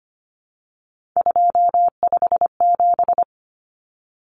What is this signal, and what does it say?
Morse code, heard as a keyed tone, about 25 wpm, 700 Hz. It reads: 257
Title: Common Cut Numbers - 3-Digit 25wpm